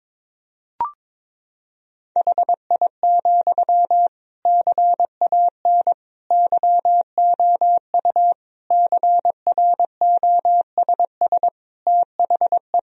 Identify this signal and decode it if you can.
Morse code, heard as a keyed tone, about 22 wpm, 700 Hz. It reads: HI, CAN YOU CROSS THE
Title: Sentences from Top 1000 Words - Review Part 3 - 22wpm